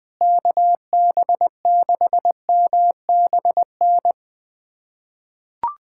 Morse code, heard as a keyed tone, about 20 wpm, 700 Hz. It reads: KB6MBN
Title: USA Callsign DE Callsign 20wpm